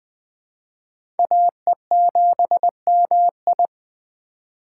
Morse code, heard as a keyed tone, about 20 wpm, 700 Hz. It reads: AE7MI
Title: USA Call Signs - 5 Characters - 4x Character Spacing - 20wpm